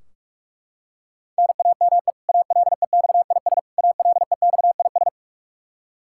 Morse code, heard as a keyed tone, about 41 wpm, 700 Hz. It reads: NAME ALEXIS ALEXIS
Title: K1USN SST QSO Practice 45wpm